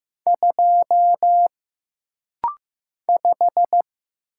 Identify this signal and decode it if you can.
Morse code, heard as a keyed tone, about 15 wpm, 700 Hz. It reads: 2 5
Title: Single Letter-Number Alt - Rapid-Fire 15wpm